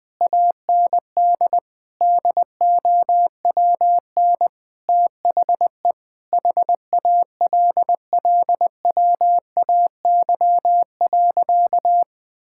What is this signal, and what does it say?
Morse code, heard as a keyed tone, about 20 wpm, 700 Hz. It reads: AND DOWN THE HALLWAY.
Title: Sentences with 4-Word Phrases - Spoken Beginning - 20wpm